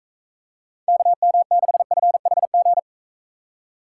Morse code, heard as a keyed tone, about 42 wpm, 700 Hz. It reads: KM6FHG